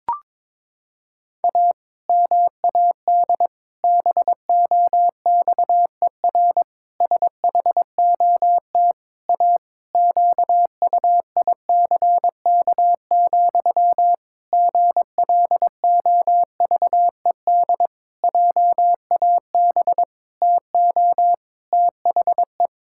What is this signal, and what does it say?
Morse code, heard as a keyed tone, about 22 wpm, 700 Hz. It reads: A MAD BOXER SHOT A QUICK, GLOVED JAB TO THE
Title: Pangram Challenge v2 - 22wpm